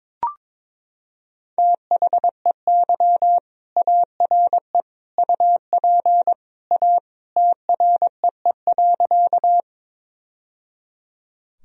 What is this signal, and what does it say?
Morse code, heard as a keyed tone, about 22 wpm, 700 Hz. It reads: THEY ARE UP A TREE.
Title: Sentences from Top 1000 Words - Review Part 1 - 22wpm